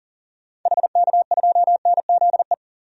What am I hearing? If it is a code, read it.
Morse code, heard as a keyed tone, about 40 wpm, 700 Hz. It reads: HK2NZE
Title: DXCC-100 Call Signs - 6 Characters - 40wpm